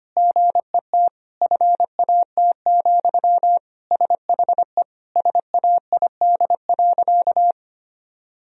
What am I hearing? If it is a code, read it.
Morse code, heard as a keyed tone, about 25 wpm, 700 Hz. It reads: GET FAT, SHE SAID.
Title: Sentences from Top 1000 Words - Review Part 4 - 25wpm